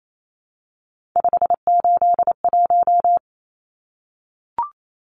Morse code, heard as a keyed tone, about 28 wpm, 700 Hz. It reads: 581
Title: Numbers - 3-Digit 28wpm